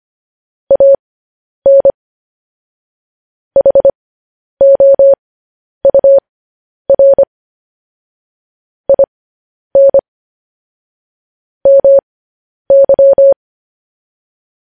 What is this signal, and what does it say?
Morse code, heard as a keyed tone, about 25 wpm, spaced as for 10 wpm, 550 Hz. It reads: AN HOUR IN MY
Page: news-headlines-in-morse-code-at-10-wpm